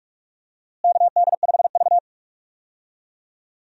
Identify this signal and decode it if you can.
Morse code, heard as a keyed tone, about 41 wpm, 700 Hz. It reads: KD5V
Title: USA Call Signs - 4 Characters - Repeated 3x - 45wpm